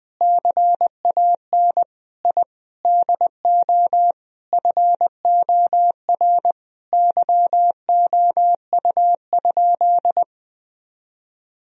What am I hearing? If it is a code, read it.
Morse code, heard as a keyed tone, about 20 wpm, 700 Hz. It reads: CAN I DO FOR YOU?
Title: Sentences from Top 1000 Words - Review Part 2 - 20wpm